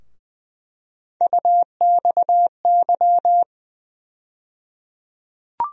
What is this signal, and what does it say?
Morse code, heard as a keyed tone, about 20 wpm, 700 Hz. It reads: UXY
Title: USA Call Sign Postfixes - 3 Letters 20wpm